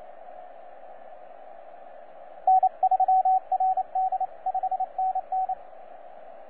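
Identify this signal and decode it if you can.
Morse code, heard as a keyed tone, about 29 wpm, 750 Hz, with radio noise. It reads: N3RD 5NN
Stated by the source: TZ6BB on 12 metres CW, from DX World.net.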